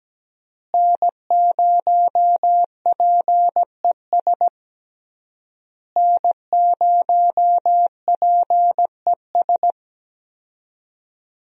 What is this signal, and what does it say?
Morse code, heard as a keyed tone, about 17 wpm, 700 Hz. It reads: N0PES N0PES
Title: USA Call Signs - 5 Characters - Repeated 3x - 17wpm